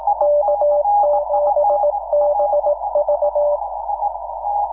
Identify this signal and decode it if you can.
Morse code, heard as a keyed tone, about 18 wpm, 590 Hz, with noise. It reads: K7BV
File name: iw0ffk / 2006 06 16 K7BV IW0FFK